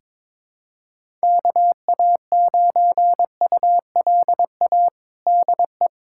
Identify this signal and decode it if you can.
Morse code, heard as a keyed tone, about 22 wpm, 700 Hz. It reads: KA9ULA DE
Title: USA Callsign DE Callsign - 6 Characters - 22wpm